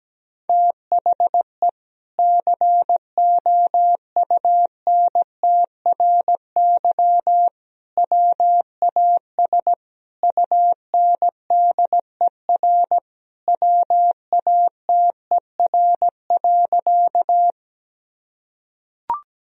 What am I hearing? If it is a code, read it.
Morse code, heard as a keyed tone, about 17 wpm, 700 Hz. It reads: THE COUNTRY WAS UNDER WATER.
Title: Sentences from Top 1000 Words - Review Part 1 - 17wpm